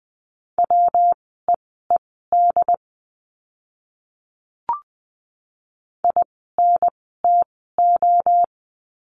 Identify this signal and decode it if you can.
Morse code, heard as a keyed tone, about 20 wpm, spaced as for 14 wpm, 700 Hz. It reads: WEED INTO
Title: 115 Lesson 19f - Letter w - Words 20wpm